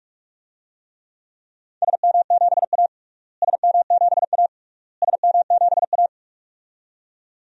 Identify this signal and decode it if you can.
Morse code, heard as a keyed tone, about 41 wpm, 700 Hz. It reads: SM7A SM7A SM7A
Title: DXCC-100 Call Signs - 4 Characters - Repeated 3x - 45wpm